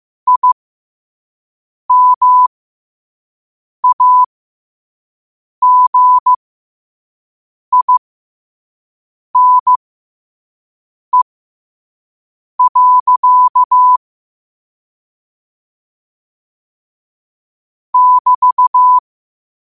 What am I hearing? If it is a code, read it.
Morse code, heard as a keyed tone, about 15 wpm, spaced as for 5 wpm, 1000 Hz. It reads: IMAGINE. <BT>
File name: qotdfiles / 5wpm